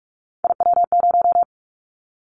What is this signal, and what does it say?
Morse code, heard as a keyed tone, about 46 wpm, 700 Hz. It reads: IW0
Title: Callsign Prefixes for Top 50 DXCC - 45wpm